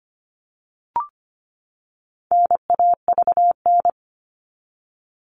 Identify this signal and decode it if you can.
Morse code, heard as a keyed tone, about 25 wpm, 700 Hz. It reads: NAVN
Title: All Cut Numbers - 4-Digit 25wpm